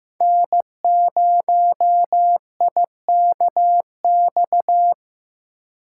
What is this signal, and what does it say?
Morse code, heard as a keyed tone, about 18 wpm, 700 Hz. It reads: N0IKX
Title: Speed Racing - USA 1x3 Call Signs 15wpm